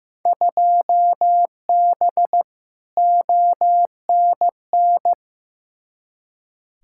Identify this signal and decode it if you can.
Morse code, heard as a keyed tone, about 15 wpm, 700 Hz. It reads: 2B ONN
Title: Field Day Exchanges Encore - 15wpm